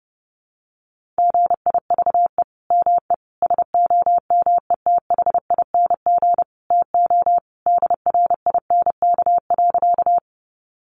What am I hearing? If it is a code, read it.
Morse code, heard as a keyed tone, about 30 wpm, 700 Hz. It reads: GIVE ME SOMETHING TO DRINK.
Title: Sentences from Top 1500 Words 30wpm